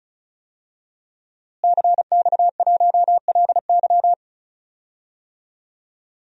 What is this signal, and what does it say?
Morse code, heard as a keyed tone, about 36 wpm, 700 Hz. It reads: CX1LY
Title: DXCC-100 Call Signs - 5 Characters - Repeated 2x - 35wpm